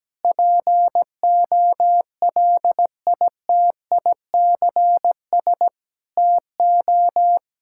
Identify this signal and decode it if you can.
Morse code, heard as a keyed tone, about 17 wpm, 700 Hz. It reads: POLITICS TO
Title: Sentences from Top 1000 Words - Review Part 6 - 17wpm